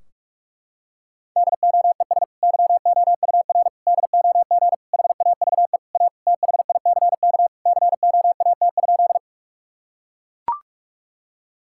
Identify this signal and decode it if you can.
Morse code, heard as a keyed tone, about 41 wpm, 700 Hz. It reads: DOES YOUR DOG HAVE A THICK COAT?
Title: Sentences from Top 1600 Words 45wpm